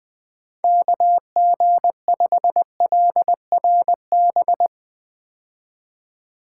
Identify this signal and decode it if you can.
Morse code, heard as a keyed tone, about 20 wpm, 700 Hz. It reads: KG5LRB
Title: 2023 POTA Activator Call Signs - 20wpm